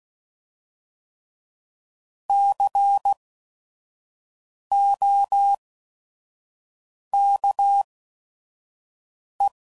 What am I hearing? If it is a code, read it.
Morse code, heard as a keyed tone, about 16 wpm, spaced as for 5 wpm, 800 Hz. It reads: COKE
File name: Prefixes / By First Letter / Cs